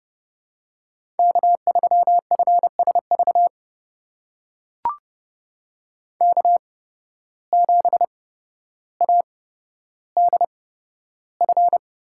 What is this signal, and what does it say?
Morse code, heard as a keyed tone, about 30 wpm, spaced as for 13 wpm, 700 Hz. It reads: K3FSV K7ADF
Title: USA Call Signs - 5 Characters - 8x Character Spacing - 30wpm